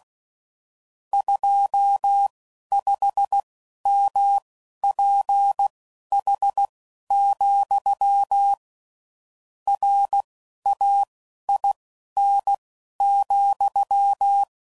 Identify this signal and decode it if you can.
Morse code, heard as a keyed tone, about 16 wpm, spaced as for 11 wpm, 800 Hz. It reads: 25MPH, RAIN,
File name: CW Code Groups @10WPM / Weather Sets / Weather 5 5